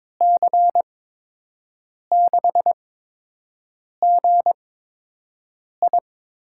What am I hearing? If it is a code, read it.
Morse code, heard as a keyed tone, about 22 wpm, spaced as for 6 wpm, 700 Hz. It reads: C6GI
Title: USA Call Signs - 5 Characters - 8x Character Spacing - 22wpm